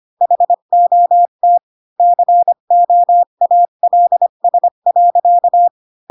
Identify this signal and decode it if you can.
Morse code, heard as a keyed tone, about 25 wpm, 700 Hz. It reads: HOT COALS.
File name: Pinocchio / Pinocchio ch5 20 25 ik1hge